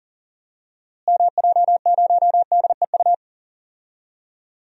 Morse code, heard as a keyed tone, about 37 wpm, 700 Hz. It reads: MJ0DEU
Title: DXCC-100 Call Signs - 6 Characters - Repeated 2x - 40wpm